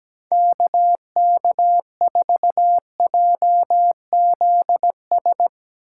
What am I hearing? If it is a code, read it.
Morse code, heard as a keyed tone, about 17 wpm, 700 Hz. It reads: KK4JZS
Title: Speed Racing - USA 2x3 Call Signs 17wpm